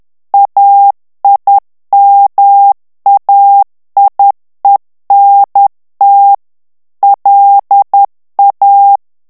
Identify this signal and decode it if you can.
Morse code, heard as a keyed tone, about 10 wpm, 800 Hz. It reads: AIMAIENT LA
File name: FRANCAIS TEXTE CLAIR 2 VITESSE 10 MOTS MINUTES